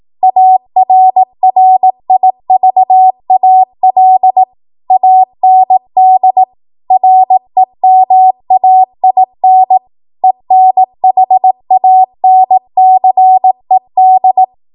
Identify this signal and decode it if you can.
Morse code, heard as a keyed tone, about 18 wpm, 750 Hz. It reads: ARRIVAL AND REMAIN ENHANCED
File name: Morse / 18WPM